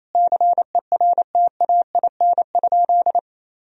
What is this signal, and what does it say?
Morse code, heard as a keyed tone, about 28 wpm, 700 Hz. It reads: CERTAIN?
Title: Sentences from Top 1000 Words - Review Part 2 - 28wpm